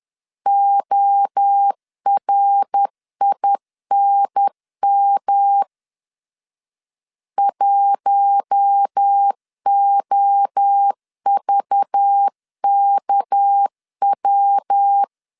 Morse code, heard as a keyed tone, about 10 wpm, 800 Hz. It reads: ORINM 1OVKW
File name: LEZIONI / 6 PKX0 / 6 PKX 10wpm 2426